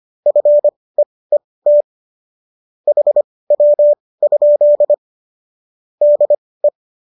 Morse code, heard as a keyed tone, about 25 wpm, spaced as for 18 wpm, 590 Hz. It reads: FEET HW? DE